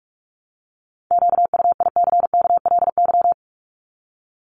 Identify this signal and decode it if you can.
Morse code, heard as a keyed tone, about 44 wpm, 700 Hz. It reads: QUICKLY
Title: LY Suffix - Top 100 Words with Base Words 45wpm